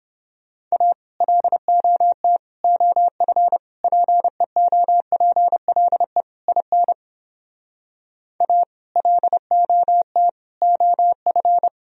Morse code, heard as a keyed tone, about 29 wpm, 700 Hz. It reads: A LOT OF PEOPLE IN A LOT OF